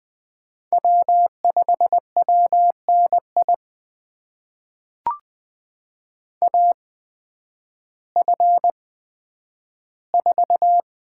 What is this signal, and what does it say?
Morse code, heard as a keyed tone, about 20 wpm, spaced as for 11 wpm, 700 Hz. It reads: W5WNI AF4
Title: USA Call Signs - 5 Characters - 8x Character Spacing - 20wpm